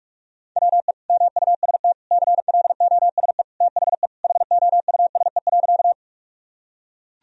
Morse code, heard as a keyed tone, about 46 wpm, 700 Hz. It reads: WE MUST CLOSE THE HOUSE.
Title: Sentences from Top 1000 Words - Review Part 1 - 45wpm